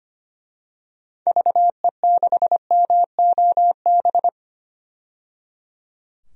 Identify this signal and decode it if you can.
Morse code, heard as a keyed tone, about 25 wpm, 700 Hz. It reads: VE6MOB
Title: DXCC-100 Call Signs - 6 Characters - 25wpm